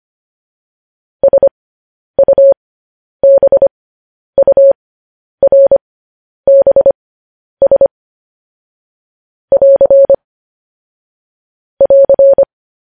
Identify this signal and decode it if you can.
Morse code, heard as a keyed tone, about 25 wpm, spaced as for 10 wpm, 550 Hz. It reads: SUBURBS <AR> <AR>
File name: podcast / News headlines-1x-25x10